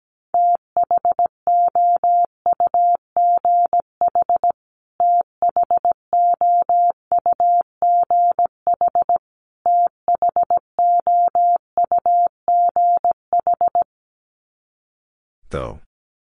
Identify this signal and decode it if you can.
Morse code, heard as a keyed tone, about 17 wpm, 700 Hz. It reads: THOUGH THOUGH THOUGH
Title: Top 500 Words Repeated 17wpm